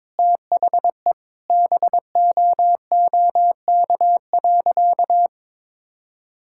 Morse code, heard as a keyed tone, about 22 wpm, 700 Hz. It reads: THE BOOK.
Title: Sentences from Top 1000 Words - Review Part 5 - 22wpm